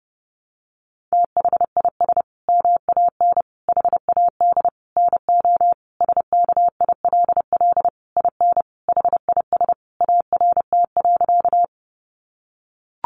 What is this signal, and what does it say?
Morse code, heard as a keyed tone, about 30 wpm, 700 Hz. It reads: THIS MAN HAD NO SKILL IN HIS ART.
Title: Sentences from Top 1200 Words 30wpm